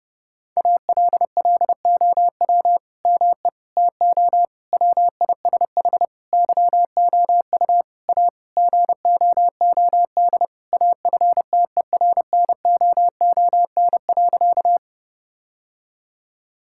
Morse code, heard as a keyed tone, about 30 wpm, 700 Hz. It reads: ALLOW ME TO WISH YOU A GOOD AFTERNOON.
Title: Sentences from Top 1600 Words 30wpm